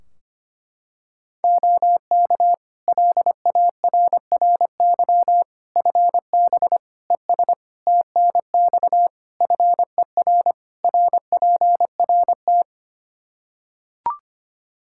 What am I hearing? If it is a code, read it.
Morse code, heard as a keyed tone, about 25 wpm, 700 Hz. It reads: OK LARRY FB ES TNX FER RPRT
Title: Practice QSO - Exchange 2 - Info - 25wpm